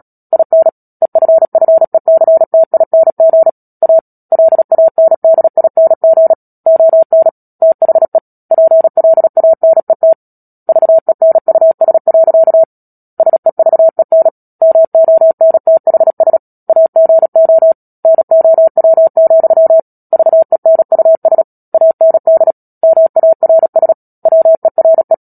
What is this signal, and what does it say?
Morse code, heard as a keyed tone, about 37 wpm, 650 Hz. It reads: IN EFFECTING A LANDING ON THE PLANET VENUS. SEVEN MONTHS AGO NOW, VENUS AND MARS WERE
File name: Book Two / 10-Chapter 10 - Epilogue - 36 WPM